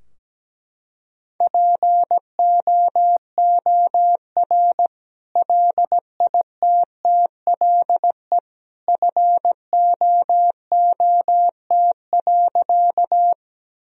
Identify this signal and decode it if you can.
Morse code, heard as a keyed tone, about 17 wpm, 700 Hz. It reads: POOR LITTLE FOOT.
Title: Sentences from Top 1000 Words - Review Part 3 - 17wpm